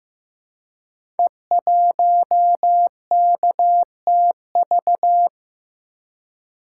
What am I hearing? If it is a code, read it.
Morse code, heard as a keyed tone, about 15 wpm, 700 Hz. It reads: E1KTV
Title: Code Groups - 5 Letter:Numbers - 15wpm